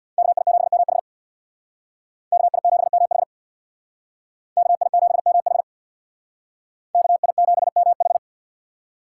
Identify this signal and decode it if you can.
Morse code, heard as a keyed tone, about 63 wpm, 690 Hz. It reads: KI7MH KI7MH KI7MH KI7MH